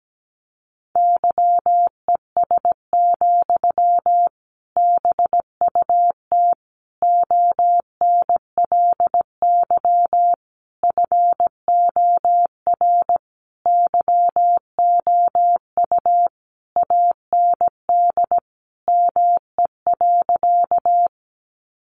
Morse code, heard as a keyed tone, about 17 wpm, 700 Hz. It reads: YES, BUT ONLY FOR YOU AND ME.
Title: Sentences from Top 500 Words 17wpm